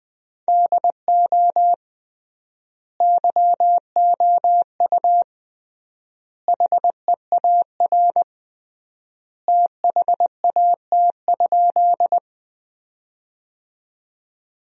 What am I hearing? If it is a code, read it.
Morse code, heard as a keyed tone, about 20 wpm, 700 Hz. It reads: DO YOU HEAR THAT?
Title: 3x Word-Spacing - Sentences from Top 300 Words - 20wpm